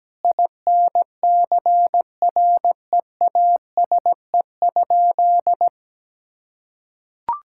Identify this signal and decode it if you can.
Morse code, heard as a keyed tone, about 17 wpm, 700 Hz. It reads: INCREASE?
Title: Sentences from Top 1000 Words - Review Part 5 - 17wpm